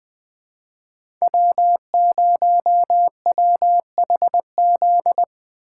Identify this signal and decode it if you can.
Morse code, heard as a keyed tone, about 20 wpm, 700 Hz. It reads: W0WHZ
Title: USA Call Signs - 3x Character Spacing - 20wpm